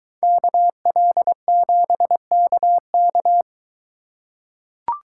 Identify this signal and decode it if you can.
Morse code, heard as a keyed tone, about 20 wpm, 700 Hz. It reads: KL7KK